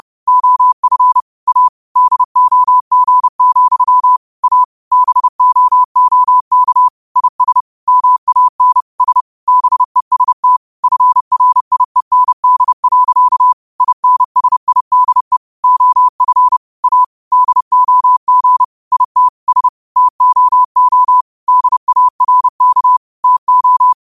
Morse code, heard as a keyed tone, about 30 wpm, 1000 Hz. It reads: OF A DOG, A BOOK IS MAN S BEST FRIEND. INSIDE OF A DOG IT S TOO DARK TO
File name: qotdfiles / 30wpm